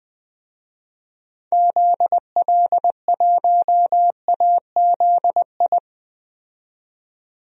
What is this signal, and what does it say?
Morse code, heard as a keyed tone, about 20 wpm, 700 Hz. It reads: ZL1AZI
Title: DXCC-100 Call Signs - 6 Characters - 20wpm